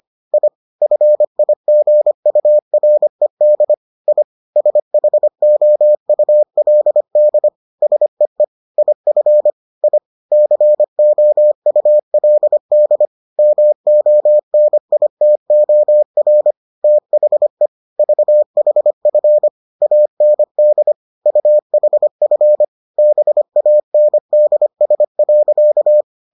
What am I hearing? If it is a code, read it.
Morse code, heard as a keyed tone, about 25 wpm, 600 Hz. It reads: I FIGURED I SHOULD SEE IF I COULD MONITOR THE VHF AND UHF BANDS.